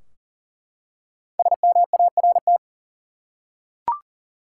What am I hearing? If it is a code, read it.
Morse code, heard as a keyed tone, about 40 wpm, 700 Hz. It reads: SMART
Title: Five Letter Words - 40wpm